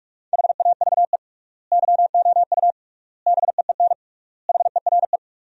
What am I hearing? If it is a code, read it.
Morse code, heard as a keyed tone, about 42 wpm, 700 Hz. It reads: HAVE YOU BEEN HERE
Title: 3x Word-Spacing - Sentences from Top 200 Words - 45wpm